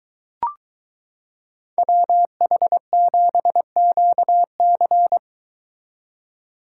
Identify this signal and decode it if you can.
Morse code, heard as a keyed tone, about 22 wpm, 700 Hz. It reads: WH7QC